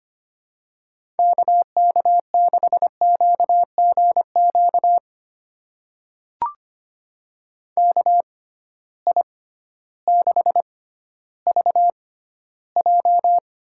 Morse code, heard as a keyed tone, about 25 wpm, spaced as for 14 wpm, 700 Hz. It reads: KK6QGQ KI6VJ
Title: USA Call Signs - 6 Characters - 6x Character Spacing - 25wpm